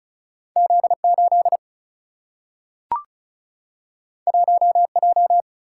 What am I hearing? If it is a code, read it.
Morse code, heard as a keyed tone, about 36 wpm, 700 Hz. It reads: Z8 1J
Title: Single-Letter Single-Number 35wpm